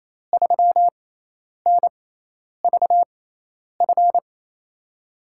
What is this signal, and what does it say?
Morse code, heard as a keyed tone, about 28 wpm, spaced as for 10 wpm, 700 Hz. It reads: 3NVF
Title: USA Call Signs - 6 Characters - 6x Character Spacing - 28wpm